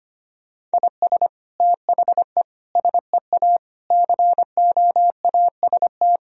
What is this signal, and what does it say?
Morse code, heard as a keyed tone, about 25 wpm, 700 Hz. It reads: IS THE SEA COAST
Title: Sentences from Top 1000 Words - Review Part 4 - 25wpm